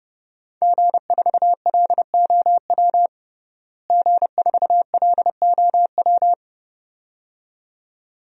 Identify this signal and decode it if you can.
Morse code, heard as a keyed tone, about 32 wpm, 700 Hz. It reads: G4LOW G4LOW
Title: DXCC-100 Call Signs - 5 Characters - Repeated 2x - 30wpm